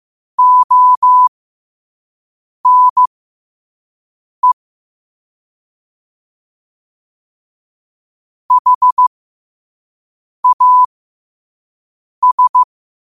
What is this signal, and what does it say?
Morse code, heard as a keyed tone, about 15 wpm, spaced as for 5 wpm, 1000 Hz. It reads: ONE HAS
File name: qotdfiles / 5wpm